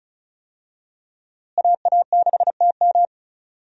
Morse code, heard as a keyed tone, about 34 wpm, 700 Hz. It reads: AA6TM
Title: USA Call Signs - 5 Characters - Repeated 3x - 35wpm